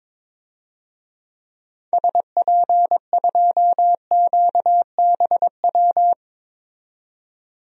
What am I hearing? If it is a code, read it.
Morse code, heard as a keyed tone, about 22 wpm, 700 Hz. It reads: SP2QBW
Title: DXCC-100 Call Signs - 6 Characters - 22wpm